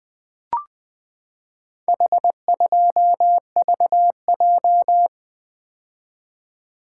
Morse code, heard as a keyed tone, about 20 wpm, 700 Hz. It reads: H2VJ
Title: Four Letter-Number Combinations - 20wpm